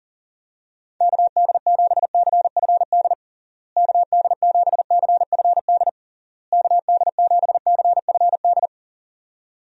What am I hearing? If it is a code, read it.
Morse code, heard as a keyed tone, about 40 wpm, 700 Hz. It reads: KD7CFD KD7CFD KD7CFD
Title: USA Call Signs - 6 Characters - Repeated 3x - 40wpm